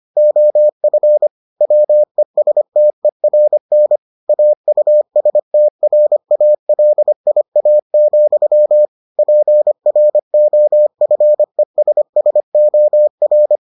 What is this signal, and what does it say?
Morse code, heard as a keyed tone, about 25 wpm, 600 Hz. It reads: OF WESTERN AUSTRALIA, PROFESSOR